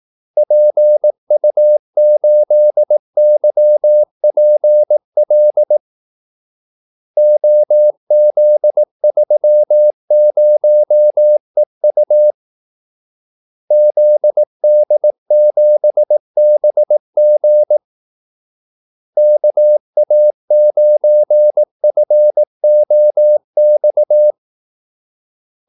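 Morse code, heard as a keyed tone, about 18 wpm, 600 Hz. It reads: PU8YPL OZ30EU ZD7BG KA9FOX
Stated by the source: Morse call-sign practice at 18 wpm.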